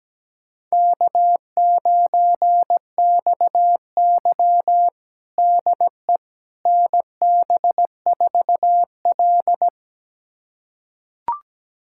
Morse code, heard as a keyed tone, about 17 wpm, 700 Hz. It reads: K9XY DE NB4L
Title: USA Callsign DE Callsign - 4 Characters - 17wpm